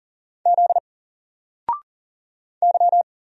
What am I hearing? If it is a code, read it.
Morse code, heard as a keyed tone, about 41 wpm, 700 Hz. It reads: Z Y
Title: Single Letter-Number Alt 40wpm